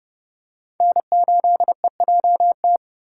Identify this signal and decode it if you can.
Morse code, heard as a keyed tone, about 29 wpm, 700 Hz. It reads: N8EJT